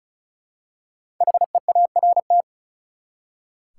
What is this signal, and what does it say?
Morse code, heard as a keyed tone, about 36 wpm, 700 Hz. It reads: HEART
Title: Five Letter Words - 35wpm